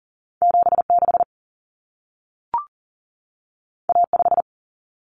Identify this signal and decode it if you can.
Morse code, heard as a keyed tone, about 40 wpm, 700 Hz. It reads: 76 A5
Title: Common Cut Numbers - 2-Digit 40wpm